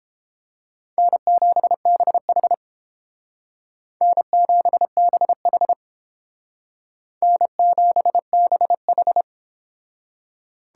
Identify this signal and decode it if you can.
Morse code, heard as a keyed tone, about 30 wpm, 700 Hz. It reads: N7BH N7BH N7BH